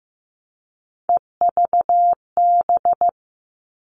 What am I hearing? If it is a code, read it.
Morse code, heard as a keyed tone, about 15 wpm, 700 Hz. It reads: EVB
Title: All Cut Numbers - 3-Digit 15wpm